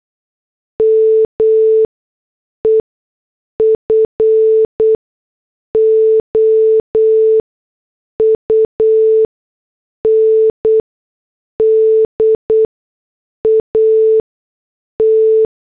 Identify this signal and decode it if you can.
Morse code, heard as a keyed tone, about 8 wpm, spaced as for 6 wpm, 440 Hz. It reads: ME FOUNDAT
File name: morse / headlines